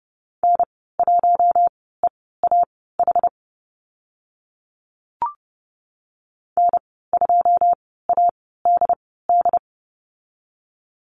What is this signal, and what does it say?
Morse code, heard as a keyed tone, about 30 wpm, spaced as for 17 wpm, 700 Hz. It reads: N1EAH N2ADD
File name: Lesson 14g - Number 2 - Call Signs 30wpm